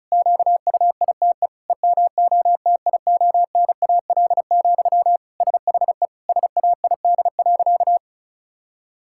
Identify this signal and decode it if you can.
Morse code, heard as a keyed tone, about 36 wpm, 700 Hz. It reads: QUITE EMOTIONAL, SHE SAID.
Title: Sentences from Top 1000 Words - Review Part 5 - 35wpm